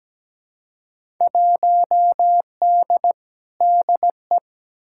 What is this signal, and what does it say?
Morse code, heard as a keyed tone, about 17 wpm, 700 Hz. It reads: 1D DE
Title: Field Day Exchanges - 17wpm